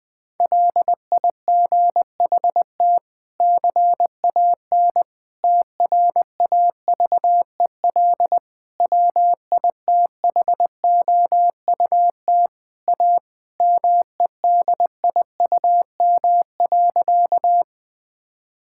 Morse code, heard as a keyed tone, about 20 wpm, 700 Hz. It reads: LIGHT CAN TRAVEL WITHOUT A MEDIUM.
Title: Sentences from Top 1000 Words - Review Part 6 - 20wpm